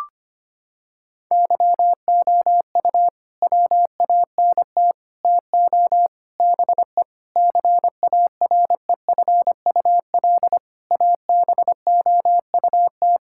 Morse code, heard as a keyed tone, about 25 wpm, 700 Hz. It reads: YOU WANT TO BE CAREFUL ABOUT
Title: Sentences from Top 1600 Words 25wpm